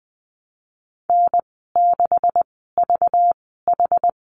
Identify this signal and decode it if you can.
Morse code, heard as a keyed tone, about 20 wpm, spaced as for 14 wpm, 700 Hz. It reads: N6VH
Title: 193 Lesson 31g - Letter v - Call Signs 20wpm